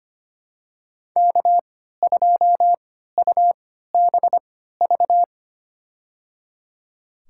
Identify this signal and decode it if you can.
Morse code, heard as a keyed tone, about 25 wpm, spaced as for 14 wpm, 700 Hz. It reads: K2UBV
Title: USA Call Signs - 5 Characters - 3x Character Spacing - 25wpm